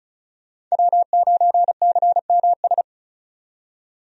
Morse code, heard as a keyed tone, about 34 wpm, 700 Hz. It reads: W9CMS
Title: USA Call Signs - 5 Characters - 35wpm